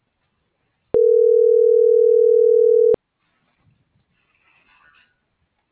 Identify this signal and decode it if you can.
Ringback tone (North America), 440 + 480 Hz.